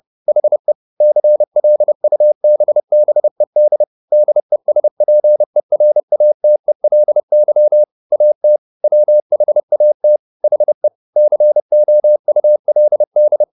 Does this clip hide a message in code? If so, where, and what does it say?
Yes: Morse code from 0.3 to 13.4 s, heard as a keyed tone, about 30 wpm, 600 Hz. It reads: HE CLUBBED DESPERATELY AT WHAT HE COULD